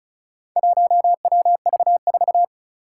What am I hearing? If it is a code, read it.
Morse code, heard as a keyed tone, about 36 wpm, 690 Hz. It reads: 1WV4
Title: Four Challenging Letter-Number Combinations 35wpm